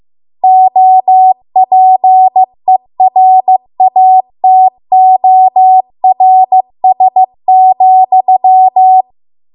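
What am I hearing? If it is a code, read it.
Morse code, heard as a keyed tone, about 15 wpm, 750 Hz. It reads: OPERATORS,
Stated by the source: Morse code practice at 15 wpm from the national amateur radio association's code archive.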